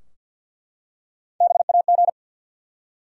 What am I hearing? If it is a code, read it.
Morse code, heard as a keyed tone, about 51 wpm, 700 Hz. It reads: BAG
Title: Three Letter Words - 50wpm